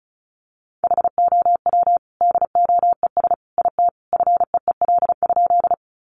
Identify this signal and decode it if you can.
Morse code, heard as a keyed tone, about 34 wpm, 700 Hz. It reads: HOW DOES IT FEEL?
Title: Sentences from Top 400 Words 35wpm